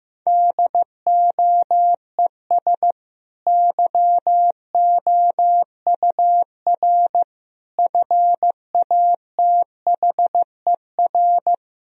Morse code, heard as a keyed tone, about 15 wpm, 700 Hz. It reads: DOES YOUR FATHER
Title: Sentences from Top 1000 Words - Review Part 3 - 15wpm